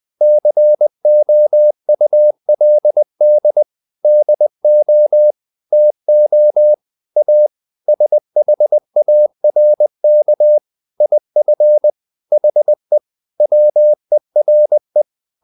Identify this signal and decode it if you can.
Morse code, heard as a keyed tone, about 20 wpm, 600 Hz. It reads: COULD DO TO A SHARK IF HE WERE
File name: the old man and the sea part 2 20wpm